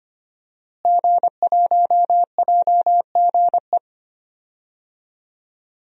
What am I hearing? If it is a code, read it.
Morse code, heard as a keyed tone, about 25 wpm, 700 Hz. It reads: G1JGE
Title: DXCC-100 Call Signs - 5 Characters - Repeated 3x - 25wpm